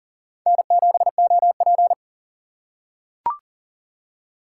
Morse code, heard as a keyed tone, about 40 wpm, 700 Hz. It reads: N7OP
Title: USA Call Signs - 4 Characters - 6x Character Spacing - 40wpm